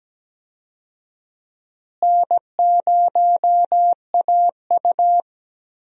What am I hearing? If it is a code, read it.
Morse code, heard as a keyed tone, about 17 wpm, 700 Hz. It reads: N0AU
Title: USA Call Signs - 4 Characters - Repeated 3x - 17wpm